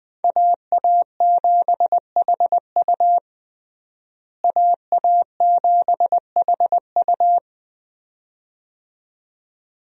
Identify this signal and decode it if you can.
Morse code, heard as a keyed tone, about 20 wpm, 700 Hz. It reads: AA7HU AA7HU
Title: USA Call Signs - 5 Characters - Repeated 2x - 20wpm